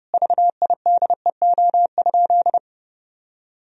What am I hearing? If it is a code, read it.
Morse code, heard as a keyed tone, about 32 wpm, 700 Hz. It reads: VIDEO?
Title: Sentences from Top 1000 Words - Review Part 4 - 30wpm